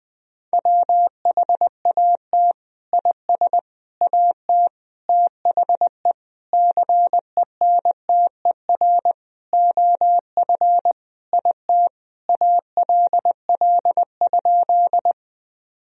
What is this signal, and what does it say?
Morse code, heard as a keyed tone, about 20 wpm, 700 Hz. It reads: WHAT IS AT THE CENTER OF IT ALL?
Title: Sentences from Top 1000 Words - Review Part 2 - 20wpm